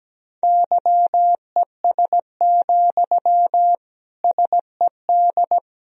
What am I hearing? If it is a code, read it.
Morse code, heard as a keyed tone, about 17 wpm, 700 Hz. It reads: YES, SED
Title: Sentences from Top 500 Words and Abbreviations 17wpm